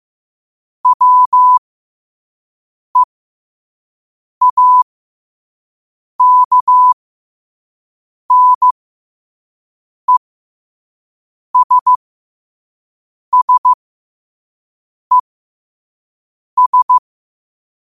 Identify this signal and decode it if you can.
Morse code, heard as a keyed tone, about 15 wpm, spaced as for 5 wpm, 1000 Hz. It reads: WEAKNESSES
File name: qotdfiles / 5wpm